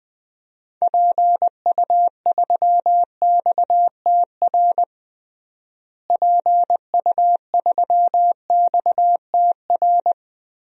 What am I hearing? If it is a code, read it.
Morse code, heard as a keyed tone, about 20 wpm, 700 Hz. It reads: PU3XTR PU3XTR
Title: DXCC-100 Call Signs - 6 Characters - Repeated 2x - 20wpm